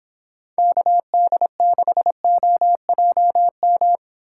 Morse code, heard as a keyed tone, about 29 wpm, 700 Hz. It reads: KD6OJM